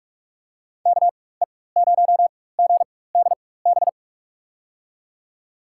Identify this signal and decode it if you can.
Morse code, heard as a keyed tone, about 42 wpm, spaced as for 21 wpm, 700 Hz. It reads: KE0GDB
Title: USA Call Signs - 4x Character Spacing - 45wpm